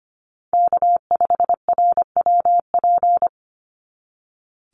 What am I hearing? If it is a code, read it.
Morse code, heard as a keyed tone, about 25 wpm, 700 Hz. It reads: K5RWP
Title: NA SOTA Chaser Callsigns 25wpm